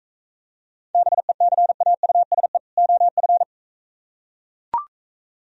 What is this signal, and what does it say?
Morse code, heard as a keyed tone, about 44 wpm, 690 Hz. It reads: BECAUSE OF